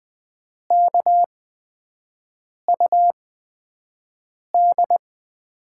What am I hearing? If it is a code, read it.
Morse code, heard as a keyed tone, about 21 wpm, spaced as for 5 wpm, 700 Hz. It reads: KUD
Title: USA Call Signs - 8x Character Spacing - 20wpm